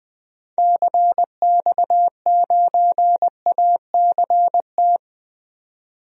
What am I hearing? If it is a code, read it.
Morse code, heard as a keyed tone, about 21 wpm, 700 Hz. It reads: CX9ACT
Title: DXCC-100 Call Signs - 6 Characters - 20wpm